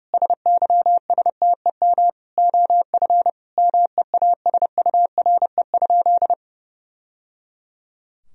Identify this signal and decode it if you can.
Morse code, heard as a keyed tone, about 30 wpm, 700 Hz. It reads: SYSTEM OF MEASURE?
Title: Sentences from Top 1000 Words - Review Part 3 - 30wpm